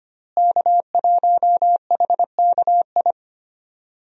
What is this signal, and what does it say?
Morse code, heard as a keyed tone, about 25 wpm, 700 Hz. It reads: K1HKI
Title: USA Call Signs - 5 Characters - Repeated 2x - 25wpm